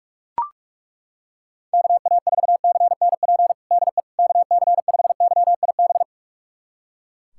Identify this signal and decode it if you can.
Morse code, heard as a keyed tone, about 41 wpm, 700 Hz. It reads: KA4CNP DE KC5YIB
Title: USA Callsign DE Callsign - 6 Characters - 45wpm